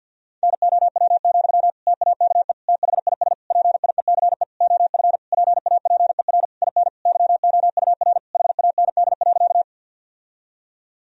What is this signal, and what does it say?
Morse code, heard as a keyed tone, about 50 wpm, 700 Hz. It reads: NOW, TAKE THIS PIECE OF PAPER IN YOUR HAND.
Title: Sentences from Top 1000 Words - Review Part 2 - 50wpm